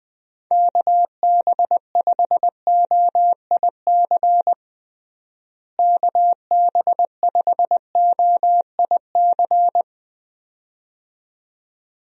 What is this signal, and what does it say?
Morse code, heard as a keyed tone, about 21 wpm, 700 Hz. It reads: KB5OIC KB5OIC
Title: USA Call Signs - 6 Characters - Repeated 2x - 20wpm